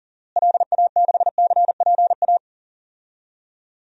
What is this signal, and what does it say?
Morse code, heard as a keyed tone, about 40 wpm, 700 Hz. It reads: LA6CPA
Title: DXCC-100 Call Signs - 6 Characters - Repeated 2x - 40wpm